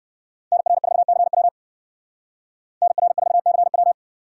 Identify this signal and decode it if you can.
Morse code, heard as a keyed tone, about 59 wpm, 700 Hz. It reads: NR4CW NR4CW